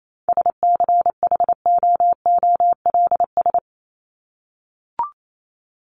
Morse code, heard as a keyed tone, about 28 wpm, 700 Hz. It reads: SCHOOLS
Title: ES:S Suffix - Top 100 Words with Base Words 28wpm